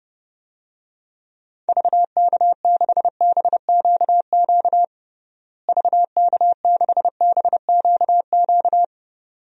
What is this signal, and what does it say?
Morse code, heard as a keyed tone, about 30 wpm, 700 Hz. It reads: VK6BQQ VK6BQQ
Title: DXCC-100 Call Signs - 6 Characters - Repeated 2x - 30wpm